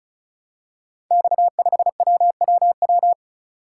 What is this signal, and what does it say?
Morse code, heard as a keyed tone, about 36 wpm, 700 Hz. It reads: X5WWW
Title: Code Groups - 5 Letter:Numbers - 35wpm